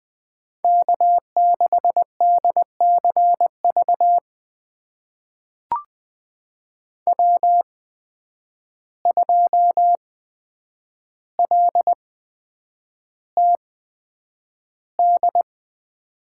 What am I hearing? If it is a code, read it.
Morse code, heard as a keyed tone, about 20 wpm, spaced as for 9 wpm, 700 Hz. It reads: K6DCV W2LTD
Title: USA Call Signs - 5 Characters - 8x Character Spacing - 20wpm